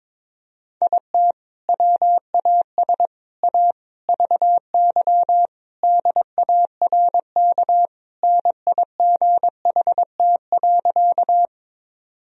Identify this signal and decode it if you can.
Morse code, heard as a keyed tone, about 22 wpm, 700 Hz. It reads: IT WAS A VY DARK NIGHT.
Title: Sentences from Top 500 Words and Abbreviations 22wpm